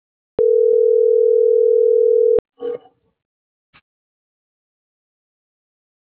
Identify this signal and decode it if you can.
Ringback tone (North America), 440 + 480 Hz.